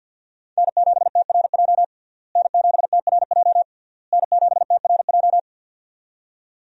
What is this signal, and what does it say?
Morse code, heard as a keyed tone, about 52 wpm, 700 Hz. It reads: N7TRJ N7TRJ N7TRJ
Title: USA Call Signs - 5 Characters - Repeated 3x - 50wpm